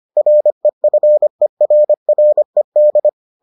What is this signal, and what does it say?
Morse code, heard as a keyed tone, about 25 wpm, 600 Hz. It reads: REFERRED